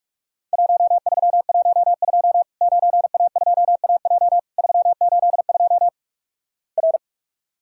Morse code, heard as a keyed tone, about 48 wpm, 700 Hz. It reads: 1312 9A2AJ 382 R
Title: Worked All Europe DX Contest - QTC Exchange - 45wpm